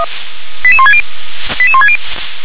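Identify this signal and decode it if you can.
Five-tone selective call (ZVEI), heard as quick runs of five tones, tones of about 70 ms. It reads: 88160 88160
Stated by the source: ZVEI five-tone selective call received on SDR.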